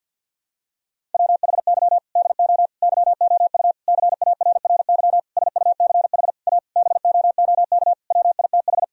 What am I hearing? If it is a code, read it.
Morse code, heard as a keyed tone, about 51 wpm, 700 Hz. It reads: WHY DO YOU CARRY SUCH A BOOK WITH
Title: Sentences from Top 1000 Words - Review Part 1 - 50wpm